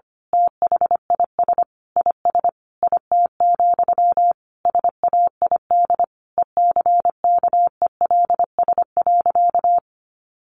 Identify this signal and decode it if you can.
Morse code, heard as a keyed tone, about 25 wpm, 700 Hz. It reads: THIS IS IT, SAID ECKELS.
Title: A Sound Of Thunder 25wpm